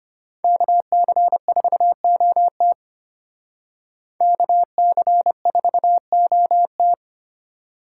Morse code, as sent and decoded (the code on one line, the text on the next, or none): -.- -.-. ....- --- - / -.- -.-. ....- --- -
KC4OT KC4OT